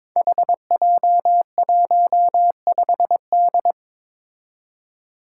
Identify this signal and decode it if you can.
Morse code, heard as a keyed tone, about 22 wpm, 700 Hz. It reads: HJ15D
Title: Five Challenging Letter-Number Combinations 22wpm